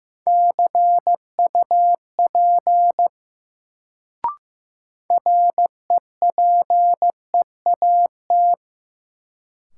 Morse code, heard as a keyed tone, about 15 wpm, 700 Hz. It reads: CUP REPEAT
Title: Sentences from Top 1600 Words - New Word Drill Alt - 15wpm